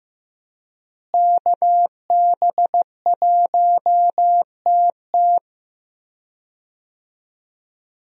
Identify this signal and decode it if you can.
Morse code, heard as a keyed tone, about 15 wpm, 700 Hz. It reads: KB1TT
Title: USA Call Signs - 5 Characters - Repeated 2x - 15wpm